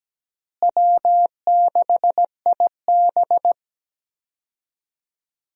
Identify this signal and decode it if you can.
Morse code, heard as a keyed tone, about 17 wpm, 700 Hz. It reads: W6IB
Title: USA Call Signs - 4 Characters - Repeated 3x - 17wpm